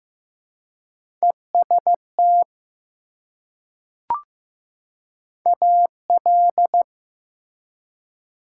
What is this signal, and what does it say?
Morse code, heard as a keyed tone, about 15 wpm, 700 Hz. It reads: EST AL
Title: Prefixes and Suffixes 15wpm